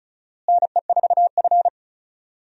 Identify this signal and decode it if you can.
Morse code, heard as a keyed tone, about 36 wpm, 700 Hz. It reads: NE4F
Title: USA Call Signs - 4 Characters - Repeated 2x - 35wpm